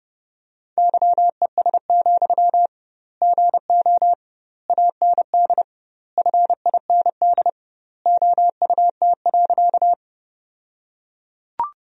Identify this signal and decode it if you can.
Morse code, heard as a keyed tone, about 30 wpm, 700 Hz. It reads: YES, GO AND FIND OUT.
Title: 2x Word-Spacing - Sentences from Top 500 Words - 30wpm